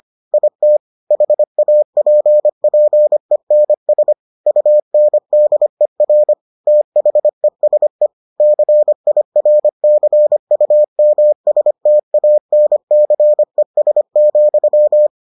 Morse code, heard as a keyed tone, about 25 wpm, 600 Hz. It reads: IT HAPPENS UNDER THESE CIRCUMSTANCES,